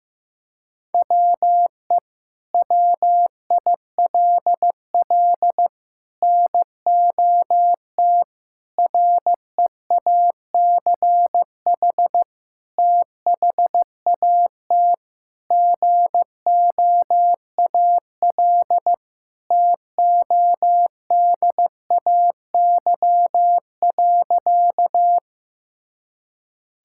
Morse code, heard as a keyed tone, about 15 wpm, 700 Hz. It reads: WE WILL NOT REACH THAT GOAL TODAY.
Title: Sentences from Top 1000 Words - Review Part 5 - 15wpm